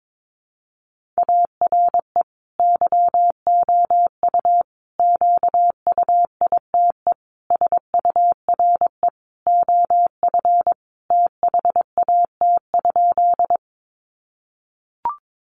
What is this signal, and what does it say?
Morse code, heard as a keyed tone, about 22 wpm, 700 Hz. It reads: ARE YOU QUITE SURE OF THAT?
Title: Sentences from Top 700 Words 22wpm